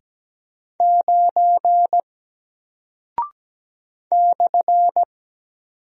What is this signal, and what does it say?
Morse code, heard as a keyed tone, about 17 wpm, 700 Hz. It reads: 9 /
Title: Single Character (Letter:Number:Punctuation:Prosign) Alt 17wpm